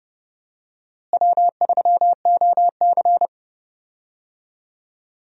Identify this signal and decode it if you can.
Morse code, heard as a keyed tone, about 32 wpm, 700 Hz. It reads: W3OC
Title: USA Call Signs - 4 Characters - Repeated 2x - 30wpm